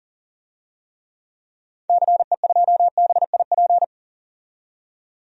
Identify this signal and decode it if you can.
Morse code, heard as a keyed tone, about 40 wpm, 700 Hz. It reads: CE2BIP
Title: DXCC-100 Call Signs - 6 Characters - 40wpm